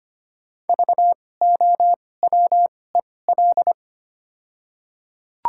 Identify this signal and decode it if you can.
Morse code, heard as a keyed tone, about 25 wpm, spaced as for 18 wpm, 700 Hz. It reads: VOWEL
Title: 2x Character Spacing - Top 500 Words - 25wpm